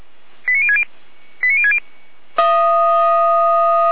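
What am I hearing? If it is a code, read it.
Five-tone selective call (ZVEI), heard as quick runs of five tones, tones of about 70 ms. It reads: 89060 89060
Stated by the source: ZVEI five-tone call 89150 received by SDR.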